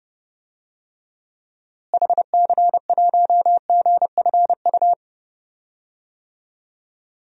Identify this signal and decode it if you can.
Morse code, heard as a keyed tone, about 29 wpm, 700 Hz. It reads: HC1GFU
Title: DXCC-100 Call Signs - 6 Characters - Repeated 3x - 30wpm